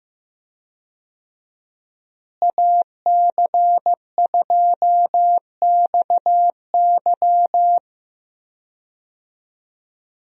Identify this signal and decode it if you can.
Morse code, heard as a keyed tone, about 15 wpm, 700 Hz. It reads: AC2XY
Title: USA Call Signs - 5 Characters - Repeated 2x - 15wpm